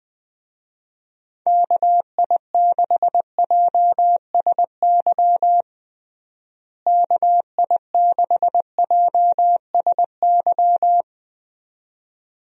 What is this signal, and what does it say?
Morse code, heard as a keyed tone, about 20 wpm, 700 Hz. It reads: KI6JSY KI6JSY
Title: USA Call Signs - 6 Characters - Repeated 2x - 20wpm